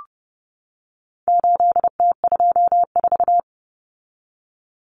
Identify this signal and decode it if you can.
Morse code, heard as a keyed tone, about 30 wpm, 700 Hz. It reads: 8T24
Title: Common Cut Numbers - 4-Digit 30wpm